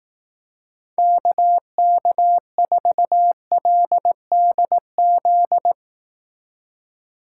Morse code, heard as a keyed tone, about 18 wpm, 700 Hz. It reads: KK4LDZ
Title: Speed Racing - USA 2x3 Call Signs 15wpm